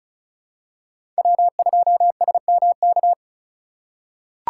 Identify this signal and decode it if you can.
Morse code, heard as a keyed tone, about 34 wpm, 700 Hz. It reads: W2SMK
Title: USA Call Signs - 5 Characters - 6x Character Spacing - 35wpm